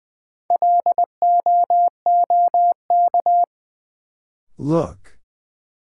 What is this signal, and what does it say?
Morse code, heard as a keyed tone, about 20 wpm, 700 Hz. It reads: LOOK
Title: Sentences from Top 100 Words - New Word Drill - 20wpm